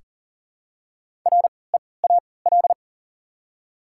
Morse code, heard as a keyed tone, about 40 wpm, spaced as for 23 wpm, 700 Hz. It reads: REAL
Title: 3x Character Spacing - Top 300 Words - 40wpm